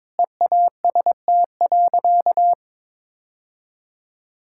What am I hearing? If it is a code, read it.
Morse code, heard as a keyed tone, about 22 wpm, 700 Hz. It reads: EAST.
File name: practice / 22wpm / Sentences with 3-Word Phrases - 22wpm